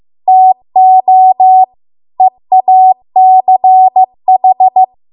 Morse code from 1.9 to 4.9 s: EACH